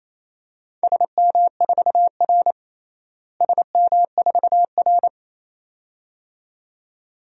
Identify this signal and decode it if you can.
Morse code, heard as a keyed tone, about 28 wpm, 700 Hz. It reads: SM4R SM4R
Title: DXCC-100 Call Signs - 4 Characters - Repeated 2x - 28wpm